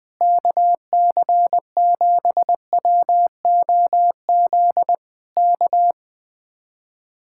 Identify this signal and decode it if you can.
Morse code, heard as a keyed tone, about 20 wpm, 700 Hz. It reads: KC7WOZ K
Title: Speed-Racing Common QSO Phrases - 20wpm